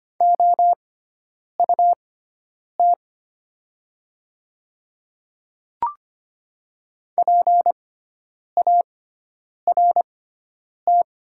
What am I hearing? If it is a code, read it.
Morse code, heard as a keyed tone, about 25 wpm, spaced as for 9 wpm, 690 Hz. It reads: OUT PART
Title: 6x Character Spacing - Top 200 Words - 25wpm